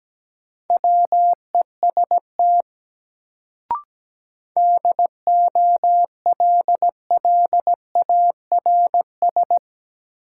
Morse code, heard as a keyed tone, about 17 wpm, 700 Hz. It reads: WEST DOLLARS
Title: Sets of 5 Words - New Word Drill - 17wpm